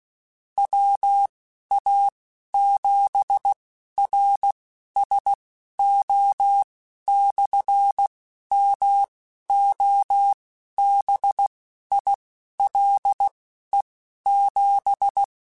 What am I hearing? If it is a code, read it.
Morse code, heard as a keyed tone, about 16 wpm, spaced as for 11 wpm, 800 Hz. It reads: WA7RSO/MOBILE7
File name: (spaced) / QSOs Spaced-mp3s / WA7QOX 1 @15 sp